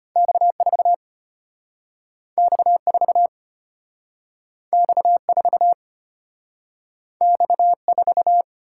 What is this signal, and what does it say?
Morse code, heard as a keyed tone, about 34 wpm, 700 Hz. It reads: X4 X4 X4 X4